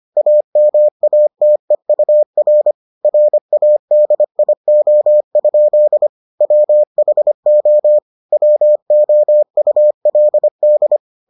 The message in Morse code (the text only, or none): AMATEUR RADIO? WHO WOULD